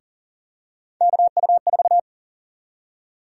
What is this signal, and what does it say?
Morse code, heard as a keyed tone, about 40 wpm, 700 Hz. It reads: KU4
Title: USA Call Sign Prefixes - 40wpm